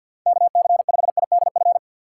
Morse code, heard as a keyed tone, about 46 wpm, 690 Hz. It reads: KC5IDF
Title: USA Call Signs - 6 Characters - 3x Character Spacing - 50wpm